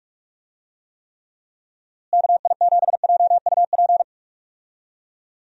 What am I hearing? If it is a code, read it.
Morse code, heard as a keyed tone, about 46 wpm, 700 Hz. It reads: KI7JUP
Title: USA Call Signs - 6 Characters - Repeated 2x - 45wpm